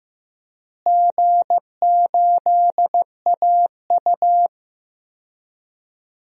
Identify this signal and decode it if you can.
Morse code, heard as a keyed tone, about 15 wpm, 700 Hz. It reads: G8AU
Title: DXCC-100 Call Signs - 4 Characters - Repeated 2x - 15wpm